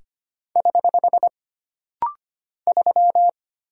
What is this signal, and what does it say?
Morse code, heard as a keyed tone, about 25 wpm, 700 Hz. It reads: <HH> 3
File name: Single Character (Letter:Number:Punctuation:Prosign) - Rapid-Fire - 25wpm